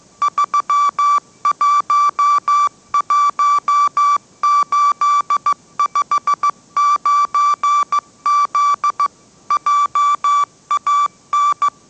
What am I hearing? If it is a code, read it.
Morse code, heard as a keyed tone, about 15 wpm, 1160 Hz. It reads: 311859Z JAN
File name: mrx / mrx065 16wpm